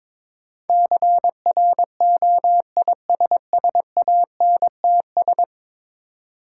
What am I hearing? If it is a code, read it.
Morse code, heard as a keyed tone, about 23 wpm, 690 Hz. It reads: CROISSANTS
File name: Difficult Words with Hints - 22wpm